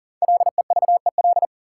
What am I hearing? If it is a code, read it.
Morse code, heard as a keyed tone, about 37 wpm, 700 Hz. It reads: LEVEL
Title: Sentences from Top 700 Words - New Word Drill - 40wpm